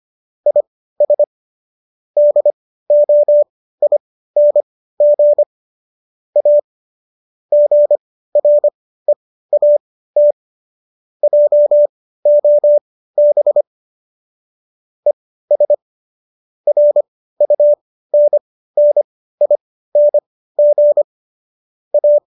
Morse code, heard as a keyed tone, about 25 wpm, spaced as for 15 wpm, 590 Hz. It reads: IS DOING A GREAT JOB ES RUNNING A